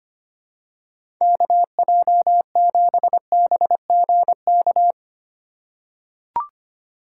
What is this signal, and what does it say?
Morse code, heard as a keyed tone, about 25 wpm, 700 Hz. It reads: KJ7BGK
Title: USA Call Signs - 4x Character Spacing - 25wpm